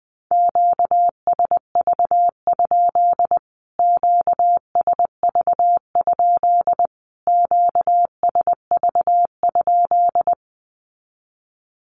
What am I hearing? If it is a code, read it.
Morse code, heard as a keyed tone, about 20 wpm, 700 Hz. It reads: QSV? QSV? QSV?
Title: Q-Codes Repeated 3x 20wpm